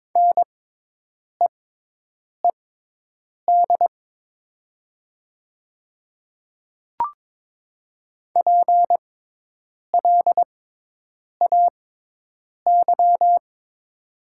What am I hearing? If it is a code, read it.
Morse code, heard as a keyed tone, about 22 wpm, spaced as for 8 wpm, 700 Hz. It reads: NEED PLAY
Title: 6x Character Spacing - Top 200 Words - 22wpm